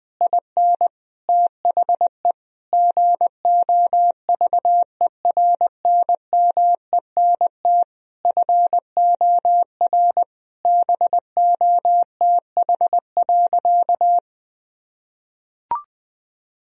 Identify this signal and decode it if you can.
Morse code, heard as a keyed tone, about 20 wpm, 700 Hz. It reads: IN THE GOVERNMENT FOR BOTH.
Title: Sentences from Top 1000 Words - Review Part 3 - 20wpm